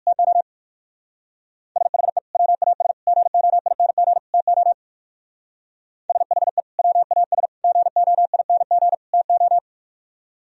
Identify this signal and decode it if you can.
Morse code, heard as a keyed tone, about 56 wpm, 700 Hz. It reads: TO SHE WAS GOING TO SHE WAS GOING TO